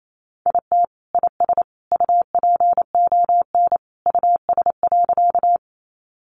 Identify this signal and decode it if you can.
Morse code, heard as a keyed tone, about 28 wpm, 700 Hz. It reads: IT IS UPON US.
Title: Sentences from Top 1100 Words 28wpm